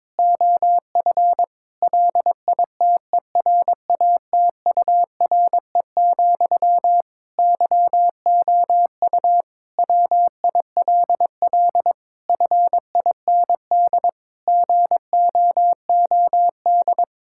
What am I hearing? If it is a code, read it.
Morse code, heard as a keyed tone, about 22 wpm, 700 Hz. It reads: OF LITERATURE, YOU WILL FIND GOOD
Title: Sentences from Top 1000 Words - Review Part 6 - 22wpm